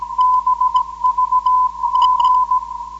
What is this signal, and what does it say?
Morse code, heard as a keyed tone, about 17 wpm, 1010 Hz. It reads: BVG
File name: 472khz web swl / 05october2012 / ok2bvg 472khz